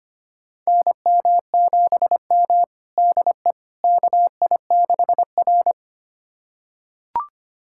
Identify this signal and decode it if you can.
Morse code, heard as a keyed tone, about 25 wpm, 700 Hz. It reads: NM7M DE KI6R
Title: USA Callsign DE Callsign - 4 Characters - 25wpm